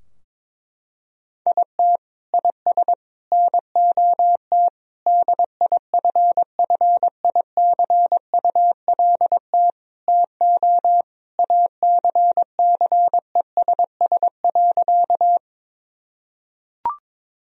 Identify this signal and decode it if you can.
Morse code, heard as a keyed tone, about 22 wpm, 700 Hz. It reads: IT IS NOT DIFFICULT TO ACCESS.
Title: Sentences from Top 1000 Words - Review Part 4 - 22wpm